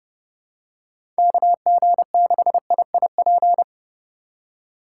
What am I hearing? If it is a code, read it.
Morse code, heard as a keyed tone, about 30 wpm, 700 Hz. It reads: KG6IIP
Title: K1USN SST QSO Practice - 30wpm